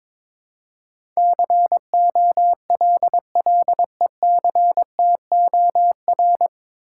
Morse code, heard as a keyed tone, about 22 wpm, 700 Hz. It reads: COLLECTOR
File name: 9 Letter Words - First 3 Letters Given - 22wpm